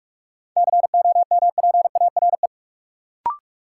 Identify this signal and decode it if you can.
Morse code, heard as a keyed tone, about 41 wpm, 700 Hz. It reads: COMPARE
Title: Sentences from Top 1300 Words - New Word Drill Alt - 45wpm